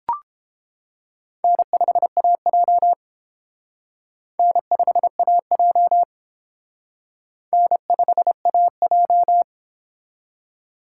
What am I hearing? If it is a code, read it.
Morse code, heard as a keyed tone, about 29 wpm, 700 Hz. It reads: N5AJ N5AJ N5AJ